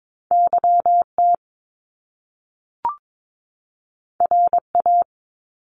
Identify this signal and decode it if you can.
Morse code, heard as a keyed tone, about 22 wpm, 700 Hz. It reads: YT RA
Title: Top Two Letter Combinations in Words 22wpm